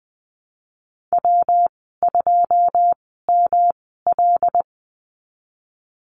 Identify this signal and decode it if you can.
Morse code, heard as a keyed tone, about 20 wpm, spaced as for 15 wpm, 700 Hz. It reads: W2ML
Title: 147 Lesson 24g - Letter y - Call Signs 20wpm